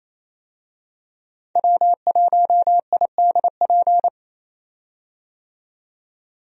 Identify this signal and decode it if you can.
Morse code, heard as a keyed tone, about 28 wpm, 700 Hz. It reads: W1IDP
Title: USA Call Signs - 5 Characters - 28wpm